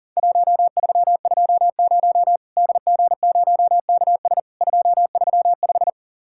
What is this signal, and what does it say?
Morse code, heard as a keyed tone, about 40 wpm, 700 Hz. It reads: 1320 DG0KS 235
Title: Worked All Europe DX Contest - QTC Exchange - 40wpm